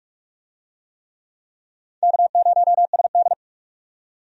Morse code, heard as a keyed tone, about 45 wpm, 700 Hz. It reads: K0SD
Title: USA Call Signs - 4 Characters - 45wpm